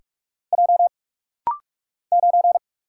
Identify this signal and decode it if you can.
Morse code, heard as a keyed tone, about 44 wpm, 700 Hz. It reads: J 9
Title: Single Letter-Number - Rapid-Fire 45wpm